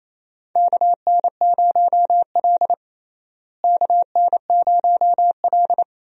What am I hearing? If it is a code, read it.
Morse code, heard as a keyed tone, about 28 wpm, 700 Hz. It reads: KN0L KN0L
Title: USA Call Signs - 4 Characters - Repeated 3x - 28wpm